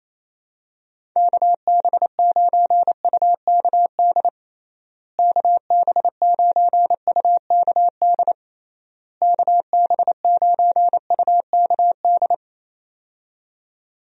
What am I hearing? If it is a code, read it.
Morse code, heard as a keyed tone, about 28 wpm, 700 Hz. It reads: KB9UKD KB9UKD KB9UKD
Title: USA Call Signs - 6 Characters - Repeated 3x - 28wpm